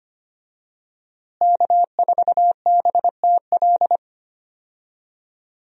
Morse code, heard as a keyed tone, about 25 wpm, 700 Hz. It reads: K4BTL
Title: USA Call Signs - 5 Characters - Repeated 3x - 25wpm